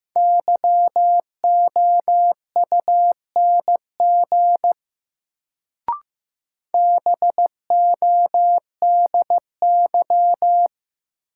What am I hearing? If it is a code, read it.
Morse code, heard as a keyed tone, about 15 wpm, 700 Hz. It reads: YOUNG BODY
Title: Sentences from Top 400 Words - New Word Drill - 15wpm